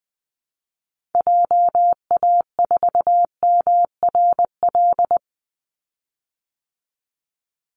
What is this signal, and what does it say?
Morse code, heard as a keyed tone, about 20 wpm, 700 Hz. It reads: JA4MRL
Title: Call Signs Repeated 3x - 20wpm